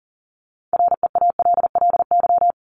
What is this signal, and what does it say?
Morse code, heard as a keyed tone, about 40 wpm, 700 Hz. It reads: REALLY
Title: LY Suffix - Top 100 Words with Base Words 40wpm